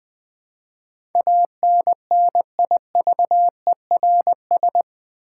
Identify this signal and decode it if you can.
Morse code, heard as a keyed tone, about 20 wpm, 700 Hz. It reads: ANNIVERS
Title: Build A Word - 4 Letters At A Time - 11-Letter Words - 20wpm